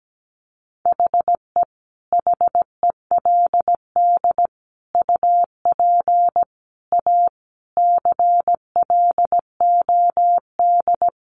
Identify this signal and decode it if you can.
Morse code, heard as a keyed tone, about 17 wpm, 700 Hz. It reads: HE HELD UP A CLOD
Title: A Sound Of Thunder 17wpm